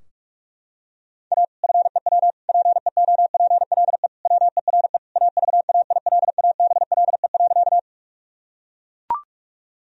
Morse code, heard as a keyed tone, about 43 wpm, 700 Hz. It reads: A FEW PEOPLE WERE AVAILABLE.
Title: Sentences from Top 1000 Words - Review Part 3 - 45wpm